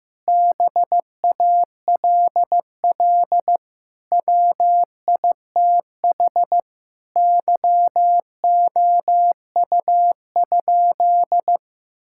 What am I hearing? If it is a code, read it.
Morse code, heard as a keyed tone, about 15 wpm, 700 Hz. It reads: BALL WITH YOU?
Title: Sentences from Top 1000 Words - Review Part 6 - 15wpm